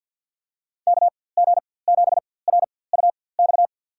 Morse code, heard as a keyed tone, about 52 wpm, spaced as for 23 wpm, 700 Hz. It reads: KG7RUX
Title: USA Call Signs - 6 Characters - 4x Character Spacing - 50wpm